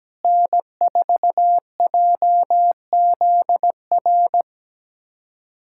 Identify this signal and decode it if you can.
Morse code, heard as a keyed tone, about 17 wpm, 700 Hz. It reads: N4JZR
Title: USA Callsign DE Callsign - 5 Characters - 17wpm